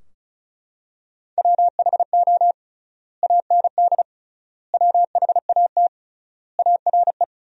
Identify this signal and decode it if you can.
Morse code, heard as a keyed tone, about 36 wpm, 700 Hz. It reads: WHO AND WHAT ARE
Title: 3x Word-Spacing - Sentences from Top 100 Words - 35wpm